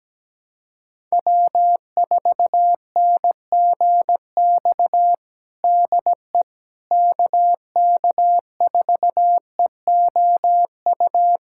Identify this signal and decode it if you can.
Morse code, heard as a keyed tone, about 17 wpm, 700 Hz. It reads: W4NGX DE KK4EOU
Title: USA Callsign DE Callsign 17wpm